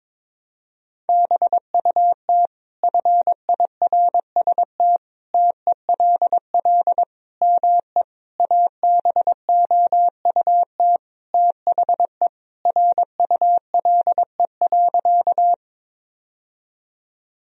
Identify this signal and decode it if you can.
Morse code, heard as a keyed tone, about 21 wpm, 700 Hz. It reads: BUT FIRST TELL ME ABOUT THE RULE.
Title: Sentences from Top 1000 Words - Review Part 2 - 22wpm